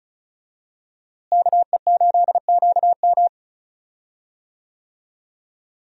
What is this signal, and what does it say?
Morse code, heard as a keyed tone, about 36 wpm, 700 Hz. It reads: KE8QM
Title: USA Call Signs - 5 Characters - Repeated 3x - 35wpm